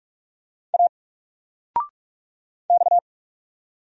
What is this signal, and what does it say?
Morse code, heard as a keyed tone, about 44 wpm, 700 Hz. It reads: A X
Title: Single Letter-Number 45wpm